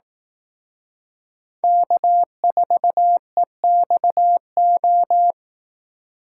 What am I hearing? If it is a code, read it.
Morse code, heard as a keyed tone, about 18 wpm, 700 Hz. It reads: K4EXO
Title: Speed Racing - USA Call Signs 15wpm